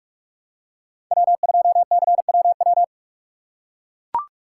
Morse code, heard as a keyed tone, about 44 wpm, 700 Hz. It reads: W2CWW
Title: USA Call Signs - 6x Character Spacing - 45wpm